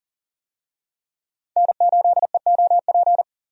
Morse code, heard as a keyed tone, about 40 wpm, 700 Hz. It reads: N8EOP
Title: USA Call Signs - 5 Characters - 40wpm